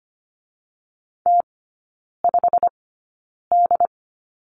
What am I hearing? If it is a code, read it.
Morse code, heard as a keyed tone, about 25 wpm, spaced as for 9 wpm, 690 Hz. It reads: T5D
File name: Lesson 15e - Number 5 - Three Characters 25wpm